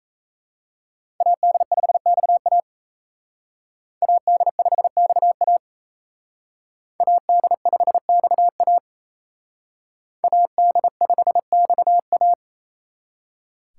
Morse code, heard as a keyed tone, about 35 wpm, 700 Hz. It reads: AD5XA AD5XA AD5XA AD5XA
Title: Speed Racing - USA 2x2 Call Signs 28wpm